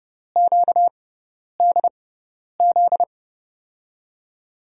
Morse code, heard as a keyed tone, about 30 wpm, spaced as for 10 wpm, 700 Hz. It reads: QDZ
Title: USA Call Signs - 6 Characters - 6x Character Spacing - 30wpm